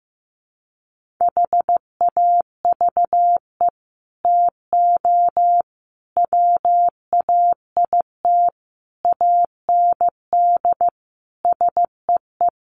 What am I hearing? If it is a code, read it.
Morse code, heard as a keyed tone, about 15 wpm, 700 Hz. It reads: HAVE TO WAIT AND SEE
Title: Sets of 5 Words Encore2 15wpm